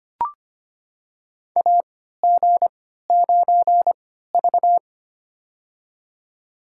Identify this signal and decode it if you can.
Morse code, heard as a keyed tone, about 25 wpm, spaced as for 14 wpm, 700 Hz. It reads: AG9V
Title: USA Call Signs - 4 Characters - 3x Character Spacing - 25wpm